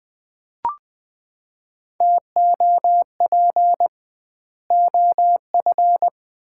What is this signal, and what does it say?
Morse code, heard as a keyed tone, about 20 wpm, 700 Hz. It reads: TOP OF
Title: 2x Word-Spacing - Sets of 3 Words - 20wpm